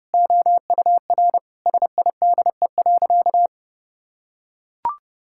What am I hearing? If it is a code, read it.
Morse code, heard as a keyed tone, about 32 wpm, 700 Hz. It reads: OUR SIDE.
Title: Sentences from Top 1000 Words - Review Part 6 - 30wpm